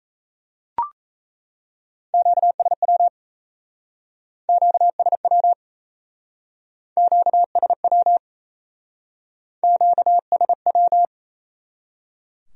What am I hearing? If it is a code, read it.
Morse code, heard as a keyed tone, about 37 wpm, 700 Hz. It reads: QSW QSW QSW QSW